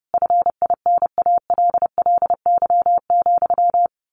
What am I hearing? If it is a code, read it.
Morse code, heard as a keyed tone, about 30 wpm, 700 Hz. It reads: FINALLY,
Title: A Sound Of Thunder 30wpm